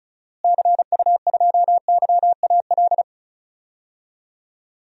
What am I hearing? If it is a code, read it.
Morse code, heard as a keyed tone, about 36 wpm, 700 Hz. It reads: CU2YAL
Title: DXCC-100 Call Signs - 6 Characters - Repeated 2x - 35wpm